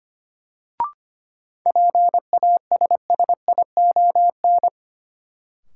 Morse code, heard as a keyed tone, about 25 wpm, 700 Hz. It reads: PASSION
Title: Sentences from Top 1300 Words - New Word Drill - 25wpm